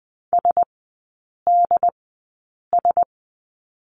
Morse code, heard as a keyed tone, about 20 wpm, spaced as for 8 wpm, 700 Hz. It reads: SDS
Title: 067 Lesson 12e - Letter d - Three Characters 20wpm